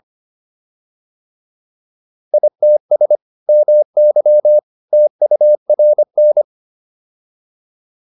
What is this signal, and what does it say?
Morse code, heard as a keyed tone, about 25 wpm, 600 Hz. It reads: ITS MY TURN
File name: PR302 25